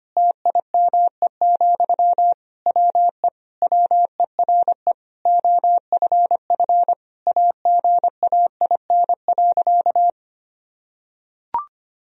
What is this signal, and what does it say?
Morse code, heard as a keyed tone, about 25 wpm, 700 Hz. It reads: TIME, WE WERE OFF AGAIN.
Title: Sentences from Top 1000 Words - Review Part 2 - 25wpm